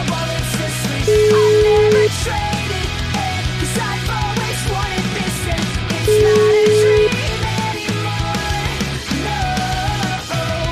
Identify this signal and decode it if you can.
Ringback tone (Europe, 425 Hz), with noise.